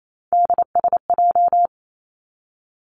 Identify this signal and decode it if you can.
Morse code, heard as a keyed tone, about 28 wpm, 700 Hz. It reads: DSJ
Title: Three Challenging Letter-Number Combinations 28wpm